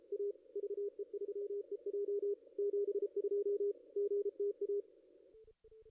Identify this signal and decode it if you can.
Morse code, heard as a keyed tone, about 32 wpm, 400 Hz. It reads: A VE3EJ 72 GTA
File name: contests / ss-cw-k5tr-op-n5rz -1600